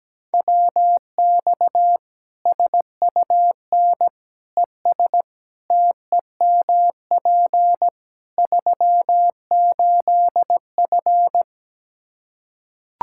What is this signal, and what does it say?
Morse code, heard as a keyed tone, about 17 wpm, 700 Hz. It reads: WX SUN ES TEMP 38F
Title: Practice QSO - Exchange 2 - Info - 17wpm